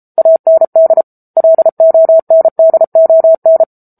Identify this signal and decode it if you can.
Morse code, heard as a keyed tone, about 34 wpm, 650 Hz. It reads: AND LONDON